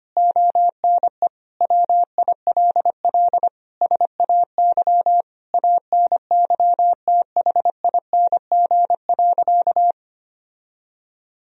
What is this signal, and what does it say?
Morse code, heard as a keyed tone, about 25 wpm, 700 Hz. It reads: ONE WILL SAY ANYTHING.
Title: Sentences from Top 1000 Words - Review Part 2 - 25wpm